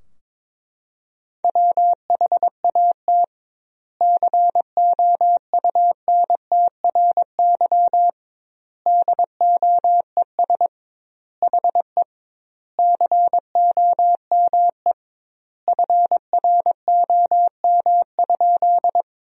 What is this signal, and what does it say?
Morse code, heard as a keyed tone, about 22 wpm, 700 Hz. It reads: WHAT COUNTRY DOES HE COME FROM?
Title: 2x Word-Spacing - Sentences from Top 300 Words - 22wpm